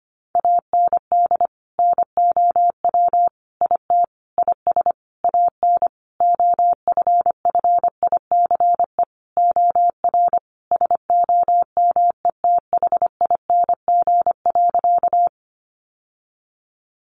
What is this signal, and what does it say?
Morse code, heard as a keyed tone, about 25 wpm, 700 Hz. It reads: AND NOW IT IS AN OFFICE OR SOMETHING.
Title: Sentences from Top 1000 Words 25wpm